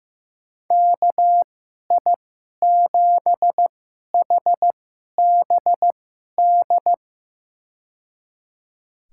Morse code, heard as a keyed tone, about 15 wpm, spaced as for 11 wpm, 700 Hz. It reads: KI7HBD
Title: USA Call Signs - 2x Character Spacing - 15wpm